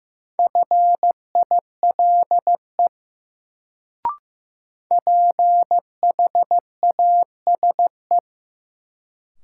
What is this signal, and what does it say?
Morse code, heard as a keyed tone, about 15 wpm, 700 Hz. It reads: FILE PHASE
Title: Sentences from Top 1200 Words - New Word Drill Alt - 15wpm